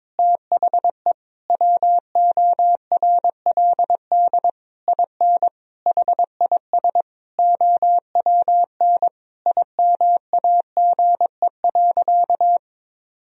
Morse code, heard as a keyed tone, about 22 wpm, 700 Hz. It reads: THE WORLD IN HIS OWN IMAGE.
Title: Sentences from Top 1000 Words - Review Part 4 - 22wpm